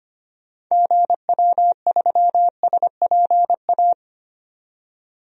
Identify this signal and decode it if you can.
Morse code, heard as a keyed tone, about 25 wpm, 700 Hz. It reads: GW3SPA
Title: DXCC-100 Call Signs - 6 Characters - 25wpm